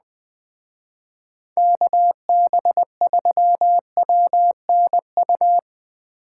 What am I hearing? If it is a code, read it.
Morse code, heard as a keyed tone, about 20 wpm, 700 Hz. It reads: KB3WNU